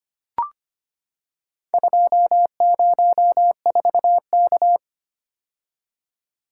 Morse code, heard as a keyed tone, about 25 wpm, 700 Hz. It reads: 204K
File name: practice / 25wpm / Four Letter-Number Combinations - 25wpm